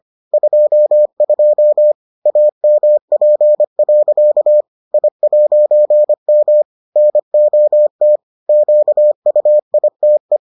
Morse code, heard as a keyed tone, about 25 wpm, 600 Hz. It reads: 22 AMP. I'M NOT QUITE